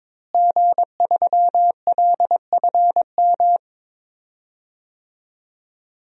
Morse code, heard as a keyed tone, about 22 wpm, 700 Hz. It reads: G3LFM
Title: DXCC-100 Call Signs - 5 Characters - Repeated 3x - 22wpm